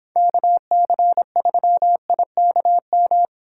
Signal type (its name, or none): Morse code